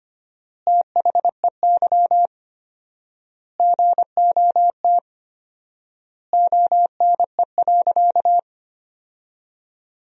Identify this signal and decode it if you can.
Morse code, heard as a keyed tone, about 25 wpm, 700 Hz. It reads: THEY GOT ONE.
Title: 4x Word-Spacing - Sentences from Top 300 Words - 25wpm